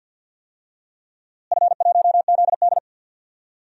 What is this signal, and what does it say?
Morse code, heard as a keyed tone, about 53 wpm, 700 Hz. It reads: F1ZD
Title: DXCC-100 Call Signs - 4 Characters - Repeated 2x - 50wpm